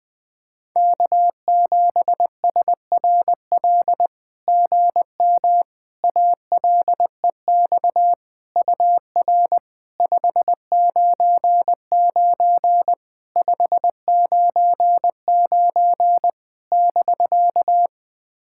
Morse code, heard as a keyed tone, about 21 wpm, 700 Hz. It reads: K7SRL GM ALEX UR 599 599 <BK>
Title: SOTA QSO Practice 20wpm